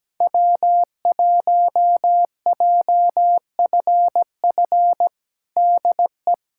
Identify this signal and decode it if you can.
Morse code, heard as a keyed tone, about 17 wpm, 700 Hz. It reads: W1JFF DE
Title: USA Callsign DE Callsign 17wpm